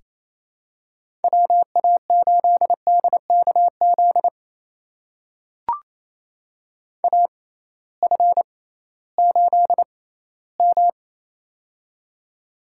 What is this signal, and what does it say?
Morse code, heard as a keyed tone, about 28 wpm, spaced as for 16 wpm, 700 Hz. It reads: WA8DKZ AF8M
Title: USA Call Signs - 6x Character Spacing - 28wpm